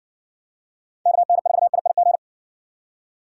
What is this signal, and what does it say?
Morse code, heard as a keyed tone, about 57 wpm, 700 Hz. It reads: KN4IIG